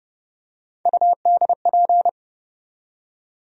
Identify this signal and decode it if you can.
Morse code, heard as a keyed tone, about 30 wpm, 700 Hz. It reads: UDP
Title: USA Call Sign Postfixes - 3 Letters 30wpm